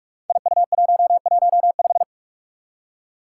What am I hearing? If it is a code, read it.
Morse code, heard as a keyed tone, about 46 wpm, 700 Hz. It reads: IU115
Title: Five Challenging Letter-Number Combinations 45wpm